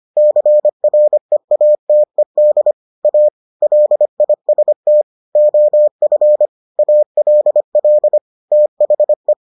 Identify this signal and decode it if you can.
Morse code, heard as a keyed tone, about 25 wpm, 600 Hz. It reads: CREATED A LIST OF ALL THE